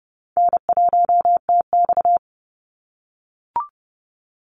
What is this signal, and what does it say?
Morse code, heard as a keyed tone, about 30 wpm, 700 Hz. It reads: N1TX
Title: Easy Call Signs - 30wpm